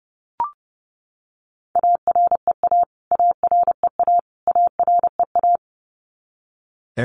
Morse code, heard as a keyed tone, about 30 wpm, 700 Hz. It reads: AREA AREA AREA
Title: Top 500 Words Repeated 30wpm